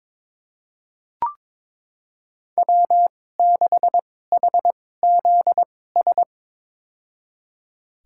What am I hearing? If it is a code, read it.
Morse code, heard as a keyed tone, about 22 wpm, spaced as for 16 wpm, 700 Hz. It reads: W6HZS
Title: USA Call Signs - 5 Characters - 2x Character Spacing - 22wpm